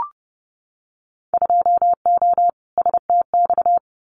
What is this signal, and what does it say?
Morse code, heard as a keyed tone, about 30 wpm, 700 Hz. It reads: 2O STX
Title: Winter Field Day Exchanges 30wpm